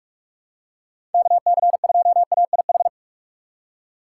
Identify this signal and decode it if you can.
Morse code, heard as a keyed tone, about 42 wpm, 700 Hz. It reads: KC2AIH
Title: USA Call Signs - 6 Characters - 2x Character Spacing - 45wpm